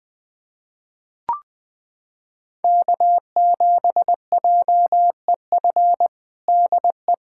Morse code, heard as a keyed tone, about 20 wpm, 700 Hz. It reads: K7JEF DE
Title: USA Callsign DE Callsign - 5 Characters - 20wpm